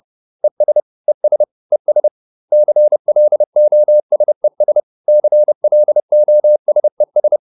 Morse code, heard as a keyed tone, about 30 wpm, 600 Hz. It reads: ES ES ES CLOSES CLOSES